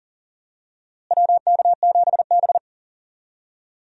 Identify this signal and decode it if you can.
Morse code, heard as a keyed tone, about 40 wpm, 700 Hz. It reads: WK7B
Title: USA Call Signs - 4 Characters - Repeated 3x - 40wpm